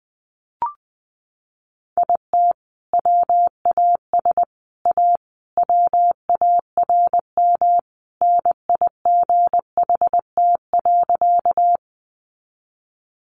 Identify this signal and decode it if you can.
Morse code, heard as a keyed tone, about 19 wpm, 700 Hz. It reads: IT WAS A WARM NIGHT.
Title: Sentences from Top 500 Words 20wpm